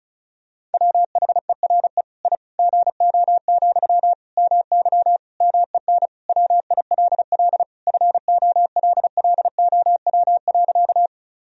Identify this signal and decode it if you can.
Morse code, heard as a keyed tone, about 34 wpm, 700 Hz. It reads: WHERE I GO, MY MEN WILL FOLLOW.
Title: Sentences from Top 1000 Words - Review Part 1 - 35wpm